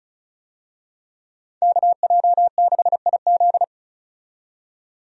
Morse code, heard as a keyed tone, about 34 wpm, 700 Hz. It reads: KJ6IZ
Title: USA Call Signs - 5 Characters - 35wpm